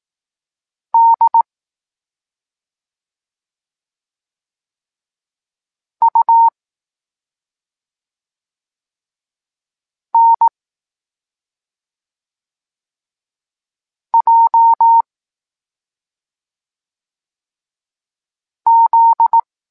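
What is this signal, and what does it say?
Morse code, heard as a keyed tone, about 18 wpm, 920 Hz. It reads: D U N J Z
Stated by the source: Evenly spaced Morse lesson text 21 for learning letters G and Z, from SP9MOA Niepołomicki Klub Krótkofalowców.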